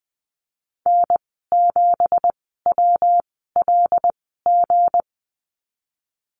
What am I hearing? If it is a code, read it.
Morse code, heard as a keyed tone, about 20 wpm, spaced as for 15 wpm, 700 Hz. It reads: N7WLG
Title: 186 Lesson 30g - Letter b - Call Signs 20wpm